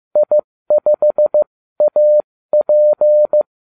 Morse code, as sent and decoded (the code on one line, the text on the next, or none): .. ..... .- .--.
I5AP